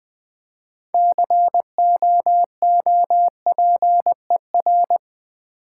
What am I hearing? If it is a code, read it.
Morse code, heard as a keyed tone, about 20 wpm, 700 Hz. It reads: COOPER
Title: Build A Word - 3 Letters At A Time - 9-Letter Words - 20wpm